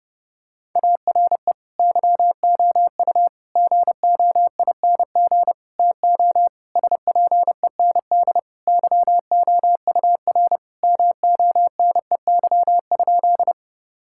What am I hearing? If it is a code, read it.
Morse code, heard as a keyed tone, about 30 wpm, 700 Hz. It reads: ARE YOU GOING TO SPEND YOUR MONEY?
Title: Sentences from Top 1000 Words - Review Part 5 - 30wpm